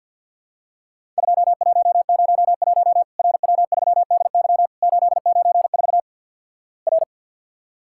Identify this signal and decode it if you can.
Morse code, heard as a keyed tone, about 50 wpm, 700 Hz. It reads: 2101 RW3DY 894 R
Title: Worked All Europe DX Contest - QTC Exchange with Repeats - 50wpm